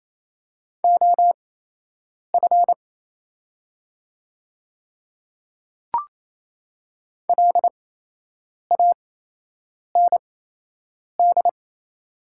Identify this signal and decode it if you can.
Morse code, heard as a keyed tone, about 28 wpm, spaced as for 8 wpm, 700 Hz. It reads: OF LAND
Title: 8x Character Spacing - Top 300 Words - 28wpm